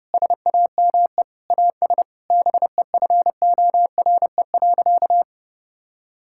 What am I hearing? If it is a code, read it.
Morse code, heard as a keyed tone, about 28 wpm, 700 Hz. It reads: SAME AS BEFORE.
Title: Sentences from Top 1000 Words - Review Part 5 - 30wpm